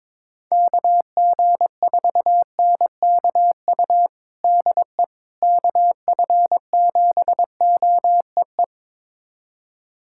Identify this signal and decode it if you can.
Morse code, heard as a keyed tone, about 22 wpm, 700 Hz. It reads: KG4NKU DE KF7OEE
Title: USA Callsign DE Callsign - 6 Characters - 22wpm